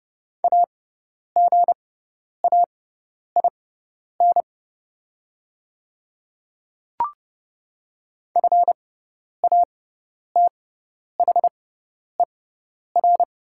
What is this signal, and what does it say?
Morse code, heard as a keyed tone, about 30 wpm, spaced as for 10 wpm, 700 Hz. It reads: AGAIN FATHER
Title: 6x Character Spacing - Top 200 Words - 30wpm